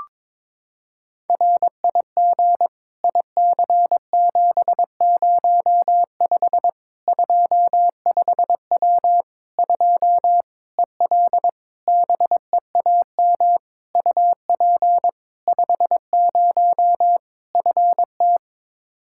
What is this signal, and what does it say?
Morse code, heard as a keyed tone, about 22 wpm, 700 Hz. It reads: RIG IC705 25W 2 EL BEAM UP 50 FT
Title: Practice QSO - Exchange 2 - Info - 22wpm